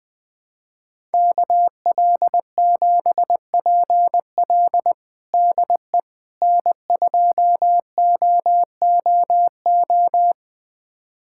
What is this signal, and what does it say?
Morse code, heard as a keyed tone, about 20 wpm, 700 Hz. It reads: KL7PL DE N2OOO
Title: USA Callsign DE Callsign - 5 Characters - 20wpm